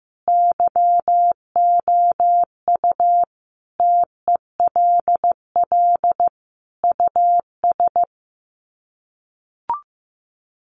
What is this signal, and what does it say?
Morse code, heard as a keyed tone, about 15 wpm, 700 Hz. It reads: YOU TELL US
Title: Sets of 5 Words Encore 15wpm